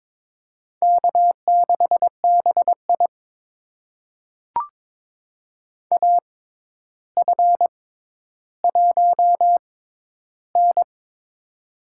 Morse code, heard as a keyed tone, about 22 wpm, spaced as for 11 wpm, 700 Hz. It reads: K6BI AF1N
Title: USA Call Signs - 4 Characters - 6x Character Spacing - 22wpm